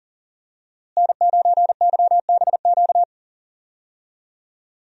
Morse code, heard as a keyed tone, about 40 wpm, 700 Hz. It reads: N9YBQ
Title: USA Call Signs - 5 Characters - Repeated 3x - 40wpm